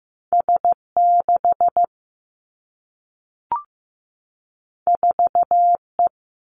Morse code, heard as a keyed tone, about 15 wpm, 700 Hz. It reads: S 6 4 E
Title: Two Challenging Letter-Number Combinations 15wpm